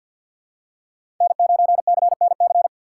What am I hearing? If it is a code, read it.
Morse code, heard as a keyed tone, about 52 wpm, 700 Hz. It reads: N9CNC
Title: USA Call Signs - 5 Characters - 6x Character Spacing - 50wpm